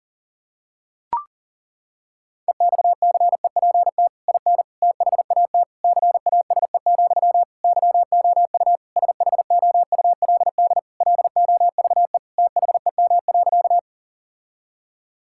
Morse code, heard as a keyed tone, about 40 wpm, 700 Hz. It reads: EXCEPT IN THAT CASE, YOU SHOULD LOVE THEM.
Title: Sentences from Top 1600 Words 40wpm